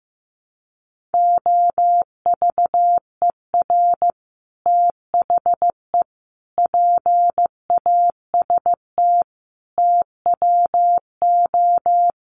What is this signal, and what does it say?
Morse code, heard as a keyed tone, about 15 wpm, 700 Hz. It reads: OVER THE PAST TWO
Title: Sets of 5 Words Encore 15wpm